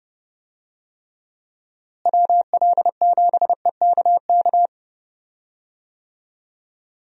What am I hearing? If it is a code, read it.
Morse code, heard as a keyed tone, about 30 wpm, 700 Hz. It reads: WL7EKK
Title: USA Call Signs - 6 Characters - Repeated 3x - 30wpm